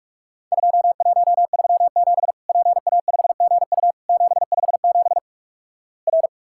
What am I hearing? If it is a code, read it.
Morse code, heard as a keyed tone, about 46 wpm, 700 Hz. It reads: 2137 PA5GU 757 R
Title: Worked All Europe DX Contest - QTC Exchange - 45wpm